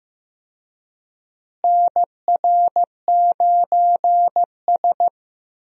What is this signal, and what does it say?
Morse code, heard as a keyed tone, about 15 wpm, 700 Hz. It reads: NR9S
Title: USA Call Signs - 4 Characters - Repeated 2x - 15wpm